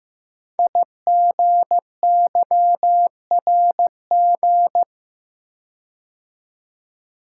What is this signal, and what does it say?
Morse code, heard as a keyed tone, about 15 wpm, 700 Hz. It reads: IGYRG
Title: Code Groups - 5 Letters - 15wpm